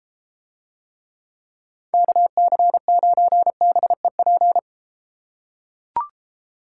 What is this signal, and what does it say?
Morse code, heard as a keyed tone, about 33 wpm, 700 Hz. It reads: KC9BEP